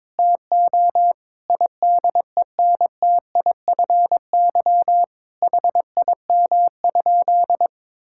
Morse code, heard as a keyed tone, about 22 wpm, 700 Hz. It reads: TO IDENTIFY HIM?
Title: Sentences from Top 1000 Words - Review Part 6 - 22wpm